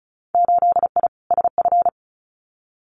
Morse code, heard as a keyed tone, about 36 wpm, 700 Hz. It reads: 8I SF
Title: Winter Field Day Exchanges Encore 35wpm